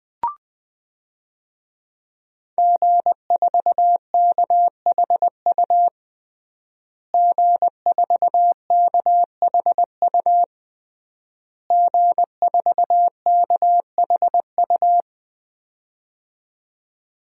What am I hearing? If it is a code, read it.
Morse code, heard as a keyed tone, about 20 wpm, 700 Hz. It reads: G4KHU G4KHU G4KHU
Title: DXCC-100 Call Signs - 5 Characters - Repeated 3x - 20wpm